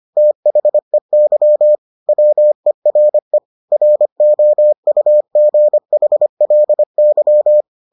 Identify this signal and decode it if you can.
Morse code, heard as a keyed tone, about 25 wpm, 600 Hz. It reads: THEY WERE ROUGHLY